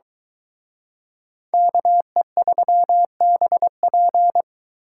Morse code, heard as a keyed tone, about 23 wpm, 700 Hz. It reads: KE3BP